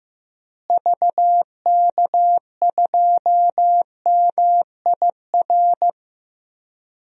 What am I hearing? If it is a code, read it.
Morse code, heard as a keyed tone, about 15 wpm, 700 Hz. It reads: VK2MIR
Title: DXCC-100 Call Signs - 6 Characters - Repeated 2x - 15wpm